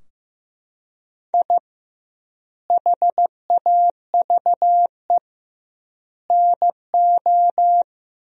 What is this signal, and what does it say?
Morse code, heard as a keyed tone, about 16 wpm, 700 Hz. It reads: I HAVE NO
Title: 2x Word-Spacing - Sentences from Top 400 Words - 15wpm